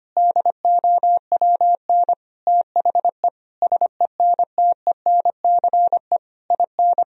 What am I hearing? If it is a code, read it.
Morse code, heard as a keyed tone, about 25 wpm, 700 Hz. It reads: DOWN THE SENTENCE IN
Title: Sentences from Top 1000 Words - Review Part 1 - 25wpm